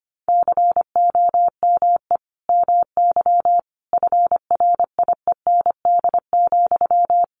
Morse code, heard as a keyed tone, about 25 wpm, 700 Hz. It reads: COME MY FRIEND,
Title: Sentences from Top 1300 Words 25wpm